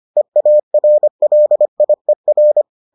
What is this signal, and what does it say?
Morse code, heard as a keyed tone, about 25 wpm, 600 Hz. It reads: EARLIER